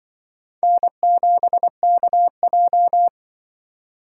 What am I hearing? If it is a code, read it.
Morse code, heard as a keyed tone, about 24 wpm, 700 Hz. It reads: N7KJ
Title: Speed Racing - USA Call Signs 20wpm